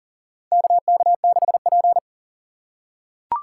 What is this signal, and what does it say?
Morse code, heard as a keyed tone, about 40 wpm, 700 Hz. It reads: KK6P
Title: USA Call Signs - 4 Characters - 4x Character Spacing - 40wpm